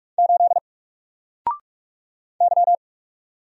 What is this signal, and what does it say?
Morse code, heard as a keyed tone, about 44 wpm, 700 Hz. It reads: 8 Y
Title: Single Letter-Number Alt 45wpm